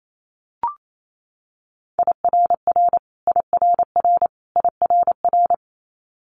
Morse code, heard as a keyed tone, about 28 wpm, 700 Hz. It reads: IRR IRR IRR
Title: Multi-letter Phonemes Repeated 3x 28wpm